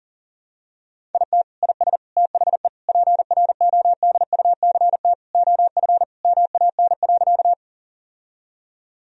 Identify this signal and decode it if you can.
Morse code, heard as a keyed tone, about 40 wpm, 700 Hz. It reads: IT IS THE PRODUCT OF MAN.
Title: Sentences from Top 1000 Words - Review Part 2 - 40wpm